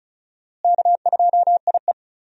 Morse code, heard as a keyed tone, about 36 wpm, 700 Hz. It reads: K2IE
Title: USA Call Signs - 4 Characters - Repeated 3x - 35wpm